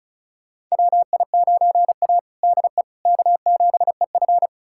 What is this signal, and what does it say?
Morse code, heard as a keyed tone, about 36 wpm, 700 Hz. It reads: WI9A DE K7EF
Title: USA Callsign DE Callsign 35wpm